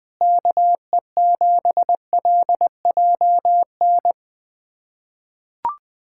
Morse code, heard as a keyed tone, about 21 wpm, 700 Hz. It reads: KE7LJN
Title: USA Callsign DE Callsign 20wpm